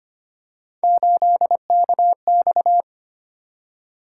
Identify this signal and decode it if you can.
Morse code, heard as a keyed tone, about 25 wpm, 700 Hz. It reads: 8KX
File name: Three Letter-Number Combinations - 25wpm